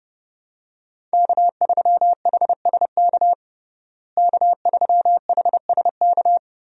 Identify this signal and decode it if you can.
Morse code, heard as a keyed tone, about 30 wpm, 700 Hz. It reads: K3HSK K3HSK
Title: USA Call Signs - 5 Characters - Repeated 2x - 30wpm